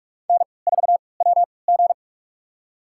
Morse code, heard as a keyed tone, about 48 wpm, spaced as for 26 wpm, 700 Hz. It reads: N4WG
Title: USA Call Signs - 3x Character Spacing - 45wpm